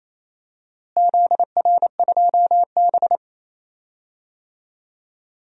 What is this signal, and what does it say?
Morse code, heard as a keyed tone, about 28 wpm, 700 Hz. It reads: ZR2B
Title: DXCC-100 Call Signs - 4 Characters - Repeated 2x - 28wpm